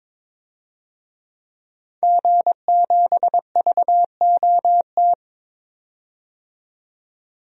Morse code, heard as a keyed tone, about 22 wpm, 700 Hz. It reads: G7VOT
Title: DXCC-100 Call Signs - 5 Characters - 22wpm